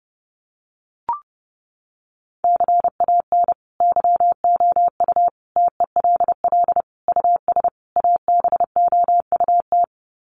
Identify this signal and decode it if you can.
Morse code, heard as a keyed tone, about 30 wpm, 700 Hz. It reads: CAN YOU TELL US ABOUT
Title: Sets of 5 Words Encore2 30wpm